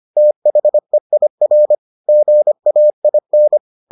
Morse code, heard as a keyed tone, about 25 wpm, 600 Hz. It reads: THEIR GAIN